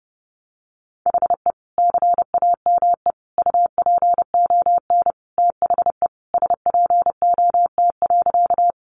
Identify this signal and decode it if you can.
Morse code, heard as a keyed tone, about 30 wpm, 700 Hz. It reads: HE CAME UPON THE SPOT.
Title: Sentences from Top 1100 Words 30wpm